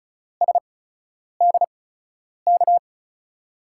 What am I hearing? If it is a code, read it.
Morse code, heard as a keyed tone, about 35 wpm, spaced as for 10 wpm, 690 Hz. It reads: SDK
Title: USA Call Signs - 5 Characters - 8x Character Spacing - 35wpm